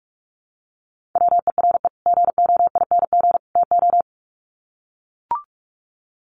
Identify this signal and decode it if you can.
Morse code, heard as a keyed tone, about 44 wpm, 700 Hz. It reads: WERE GOING TO
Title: Sets of 3 Words 45wpm